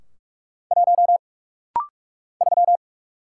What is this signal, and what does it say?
Morse code, heard as a keyed tone, about 46 wpm, 700 Hz. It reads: 1 3
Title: Single Letter-Number - Rapid-Fire 45wpm